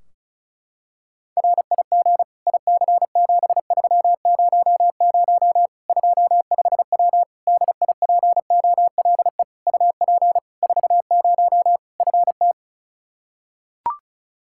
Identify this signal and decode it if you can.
Morse code, heard as a keyed tone, about 36 wpm, 700 Hz. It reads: RIG IC7300 25W DIPOLE UP 40 FT
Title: K1USN SST QSO Practice 35wpm